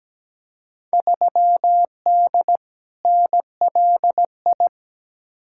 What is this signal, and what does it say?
Morse code, heard as a keyed tone, about 17 wpm, 700 Hz. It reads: 3D NLI
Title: Field Day Exchanges Encore - 17wpm